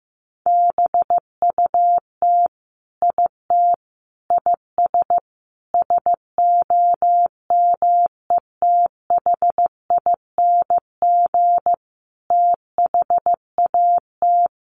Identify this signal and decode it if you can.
Morse code, heard as a keyed tone, about 15 wpm, 700 Hz. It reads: BUT IT IS SOMETHING THAT
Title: Sentences from Top 1000 Words 15wpm